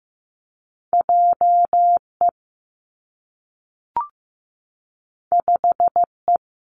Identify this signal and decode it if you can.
Morse code, heard as a keyed tone, about 15 wpm, 700 Hz. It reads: J E 5 E
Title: Two Challenging Letter-Number Combinations 15wpm